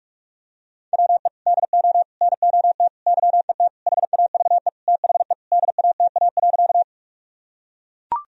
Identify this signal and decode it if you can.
Morse code, heard as a keyed tone, about 42 wpm, 700 Hz. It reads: WE DO NOT YET HAVE THE DATA.
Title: Sentences from Top 1000 Words - Review Part 3 - 45wpm